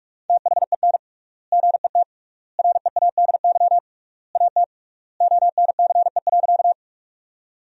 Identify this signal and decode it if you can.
Morse code, heard as a keyed tone, about 43 wpm, 700 Hz. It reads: THEN GET READY AT ONCE.
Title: 3x Word-Spacing - Sentences from Top 400 Words - 45wpm